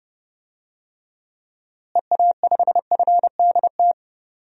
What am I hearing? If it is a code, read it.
Morse code, heard as a keyed tone, about 32 wpm, 700 Hz. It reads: EA5FDT
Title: DXCC-100 Call Signs - 6 Characters - Repeated 2x - 30wpm